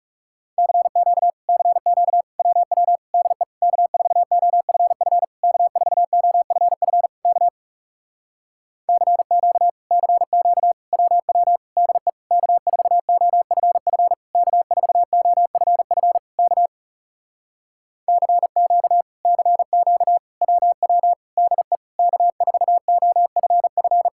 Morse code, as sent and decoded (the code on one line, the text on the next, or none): -.-. --.- / -.-. --.- / .-- .-- / -.. / -.- ....- --- ..-. ..-. / -.- ....- --- ..-. ..-. / -.- / -.-. --.- / -.-. --.- / .-- .-- / -.. / -.- ....- --- ..-. ..-. / -.- ....- --- ..-. ..-. / -.- / -.-. --.- / -.-. --.- / .-- .-- / -.. / -.- ....- --- ..-. ..-.
CQ CQ WW DE K4OFF K4OFF K CQ CQ WW DE K4OFF K4OFF K CQ CQ WW DE K4OFF